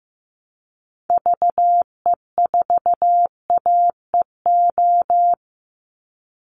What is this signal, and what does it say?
Morse code, heard as a keyed tone, about 15 wpm, 700 Hz. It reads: VE4AEO
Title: Call Signs Repeated 3x - 15wpm